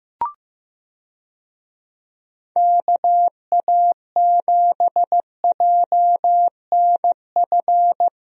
Morse code, heard as a keyed tone, about 16 wpm, 700 Hz. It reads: KA7JNF
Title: USA Call Signs - 6 Characters - Repeated 2x - 15wpm